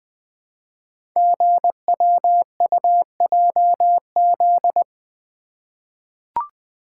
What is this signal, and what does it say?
Morse code, heard as a keyed tone, about 20 wpm, 700 Hz. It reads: GWUJZ
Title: Code Groups - 5 Letters - 20wpm